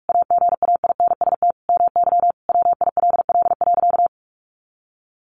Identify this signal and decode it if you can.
Morse code, heard as a keyed tone, about 46 wpm, 700 Hz. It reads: AGAINST MY WILL.
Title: Sentences from Top 1400 Words 45wpm